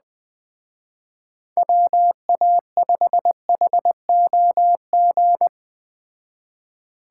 Morse code, heard as a keyed tone, about 20 wpm, 700 Hz. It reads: WA5HOG
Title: Speed Racing - USA 2x3 Call Signs 17wpm